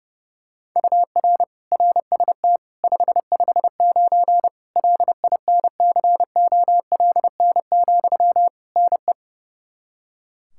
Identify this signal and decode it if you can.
Morse code, heard as a keyed tone, about 32 wpm, 700 Hz. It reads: UR RST 559 LINCOLN, NE
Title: Practice QSO - Exchange 1 - Beginning - 30wpm